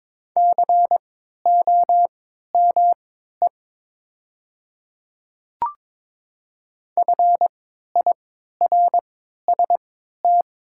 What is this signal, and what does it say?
Morse code, heard as a keyed tone, about 22 wpm, spaced as for 12 wpm, 700 Hz. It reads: COME FIRST
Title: 3x Character Spacing - Top 200 Words - 22wpm